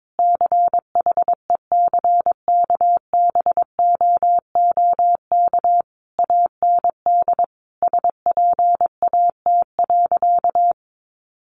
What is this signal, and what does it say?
Morse code, heard as a keyed tone, about 22 wpm, 700 Hz. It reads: CHECKBOOK AND SPAT.
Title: A Sound Of Thunder 22wpm